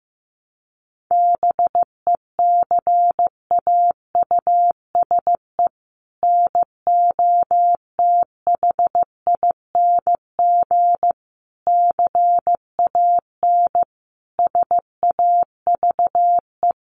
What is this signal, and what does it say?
Morse code, heard as a keyed tone, about 15 wpm, 700 Hz. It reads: BECAUSE NOTHING CAN SAVE
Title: Sentences from Top 1100 Words 15wpm